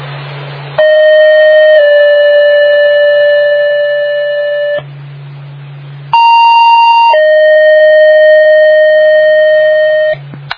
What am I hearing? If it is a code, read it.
Two-tone page, heard as a one-second tone, then a three-second tone, about 620 then 580 Hz, about 930 then 620 Hz. It is heard from 0.8 to 10.1 s.